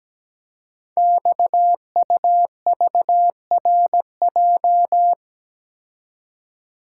Morse code, heard as a keyed tone, about 17 wpm, 700 Hz. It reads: XUVRJ
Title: Code Groups - 5 Letters - 17wpm